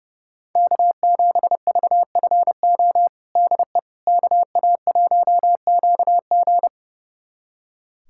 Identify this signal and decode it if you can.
Morse code, heard as a keyed tone, about 30 wpm, 700 Hz. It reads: K7VFO DE KA1QG
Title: USA Callsign DE Callsign - 5 Characters - 30wpm